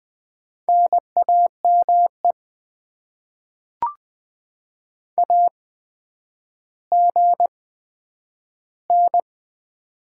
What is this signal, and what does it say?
Morse code, heard as a keyed tone, about 20 wpm, spaced as for 10 wpm, 700 Hz. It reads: NAME AGN
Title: Top 20 QSO Elements - 8x Character Spacing - 20wpm